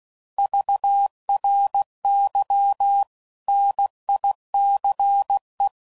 Morse code, heard as a keyed tone, about 16 wpm, 800 Hz. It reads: VRY NICE
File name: CW Code Groups @15WPM / QSOs / kc7sac